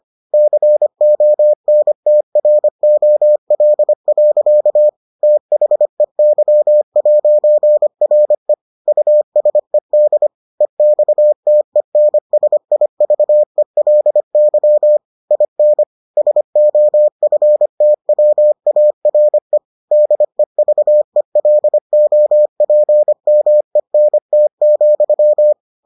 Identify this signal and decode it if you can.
Morse code, heard as a keyed tone, about 25 wpm, 600 Hz. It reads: CONTROL. THEY'RE USED EXTENSIVELY IN SOFTWARE DEVELOPMENT,